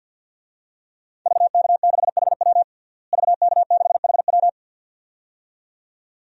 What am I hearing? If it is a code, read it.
Morse code, heard as a keyed tone, about 51 wpm, 700 Hz. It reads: VK6HW VK6HW
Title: DXCC-100 Call Signs - 5 Characters - Repeated 2x - 50wpm